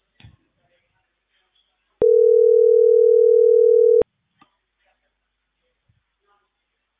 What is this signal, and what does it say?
Ringback tone (North America), 440 + 480 Hz.